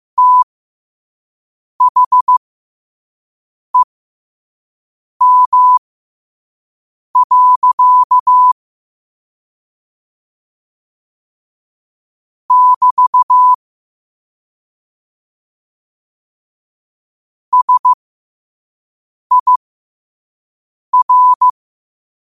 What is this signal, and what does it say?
Morse code, heard as a keyed tone, about 15 wpm, spaced as for 5 wpm, 1000 Hz. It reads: THEM. <BT> SIR
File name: qotdfiles / 5wpm